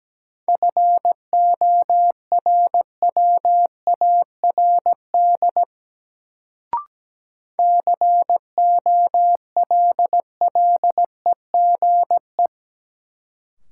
Morse code, heard as a keyed tone, about 17 wpm, 700 Hz. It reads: FORWARD COLLEGE
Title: Sentences from Top 1300 Words - New Word Drill Alt - 17wpm